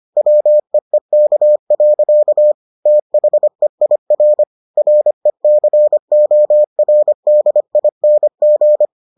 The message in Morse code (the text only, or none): WEEK. THEIR RECORDING